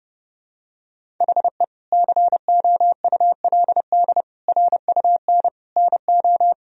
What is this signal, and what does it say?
Morse code, heard as a keyed tone, about 30 wpm, 700 Hz. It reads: HE COULD RUN NO
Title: Sentences from Top 1000 Words - Review Part 1 - 30wpm